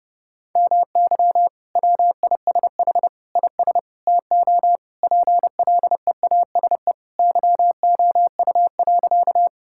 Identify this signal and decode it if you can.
Morse code, heard as a keyed tone, about 30 wpm, 700 Hz. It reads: MY WISH IS TO PLEASE YOU.
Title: Sentences from Top 1600 Words 30wpm